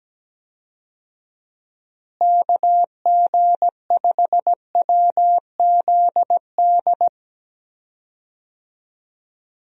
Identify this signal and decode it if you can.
Morse code, heard as a keyed tone, about 17 wpm, 700 Hz. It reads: KG5WZD
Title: USA Call Signs - 6 Characters - Repeated 2x - 17wpm